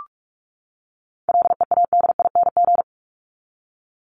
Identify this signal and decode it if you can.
Morse code, heard as a keyed tone, about 44 wpm, 700 Hz. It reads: LEADING
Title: ING Suffix - Top 100 Words with Base Words 45wpm